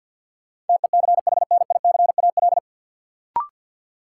Morse code, heard as a keyed tone, about 51 wpm, 700 Hz. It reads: TECHNICAL
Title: Sentences from Top 1100 Words - New Word Drill Alt - 50wpm